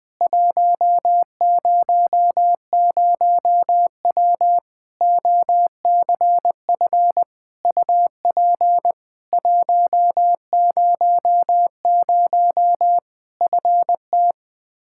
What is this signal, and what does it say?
Morse code, heard as a keyed tone, about 20 wpm, 700 Hz. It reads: 100W OCF UP 100 FT
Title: Practice QSO - Exchange 2 - Info - 20wpm